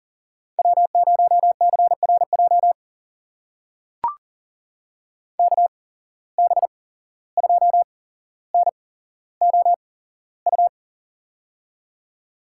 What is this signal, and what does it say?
Morse code, heard as a keyed tone, about 40 wpm, spaced as for 16 wpm, 700 Hz. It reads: W0CRJ KB2NOU
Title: USA Call Signs - 8x Character Spacing - 40wpm